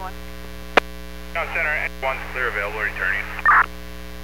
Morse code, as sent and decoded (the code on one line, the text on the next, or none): none